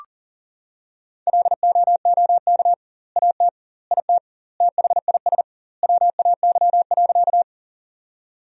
Morse code, heard as a keyed tone, about 40 wpm, 700 Hz. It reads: LOOK AT IT THIS WAY.
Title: 2x Word-Spacing - Sentences from Top 100 Words - 40wpm